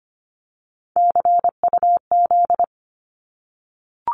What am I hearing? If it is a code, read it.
Morse code, heard as a keyed tone, about 25 wpm, 700 Hz. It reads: CUZ
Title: Abbreviations Repeated 3x - 25wpm